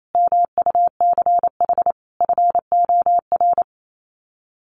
Morse code, heard as a keyed tone, about 28 wpm, 700 Hz. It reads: MUCH FOR
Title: Sets of 5 Words 28wpm